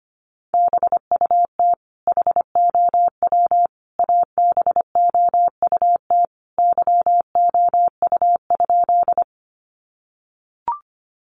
Morse code, heard as a keyed tone, about 25 wpm, 700 Hz. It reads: BUT HOW ABOUT YOU?
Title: Sentences from Top 100 Words 25wpm